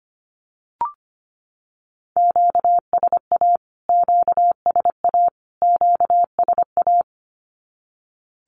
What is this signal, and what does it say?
Morse code, heard as a keyed tone, about 25 wpm, 700 Hz. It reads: QSA QSA QSA
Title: Q-Codes Repeated 3x 25wpm